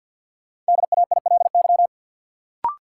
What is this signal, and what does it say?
Morse code, heard as a keyed tone, about 50 wpm, 700 Hz. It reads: DAILY
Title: Sentences from Top 1100 Words - New Word Drill - 50wpm